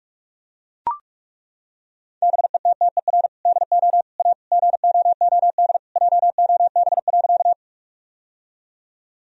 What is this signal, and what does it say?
Morse code, heard as a keyed tone, about 41 wpm, 700 Hz. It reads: BETTER DO A GOOD JOB.
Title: Sentences from Top 1000 Words - Review Part 3 - 45wpm